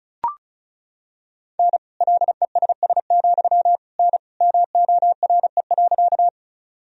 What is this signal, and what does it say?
Morse code, heard as a keyed tone, about 36 wpm, 700 Hz. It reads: N LESS, N MORE.
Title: Sentences from Top 400 Words and Abbreviations 35wpm